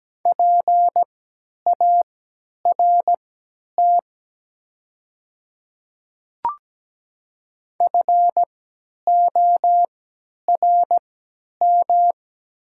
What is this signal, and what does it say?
Morse code, heard as a keyed tone, about 17 wpm, spaced as for 10 wpm, 700 Hz. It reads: PART FORM
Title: 3x Character Spacing - Top 200 Words - 17wpm